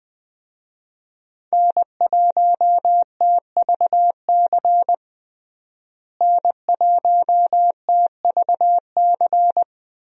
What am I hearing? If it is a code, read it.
Morse code, heard as a keyed tone, about 20 wpm, 700 Hz. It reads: N1TVC N1TVC
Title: USA Call Signs - 5 Characters - Repeated 2x - 20wpm